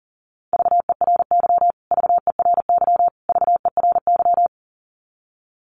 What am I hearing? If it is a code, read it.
Morse code, heard as a keyed tone, about 40 wpm, 700 Hz. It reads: VERY VERY VERY
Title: Top 500 Words Repeated 40wpm